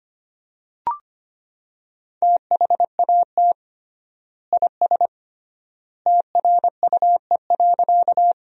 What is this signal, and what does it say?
Morse code, heard as a keyed tone, about 25 wpm, 700 Hz. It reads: THAT IS TRUE.
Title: 3x Word-Spacing - Sentences from Top 400 Words - 25wpm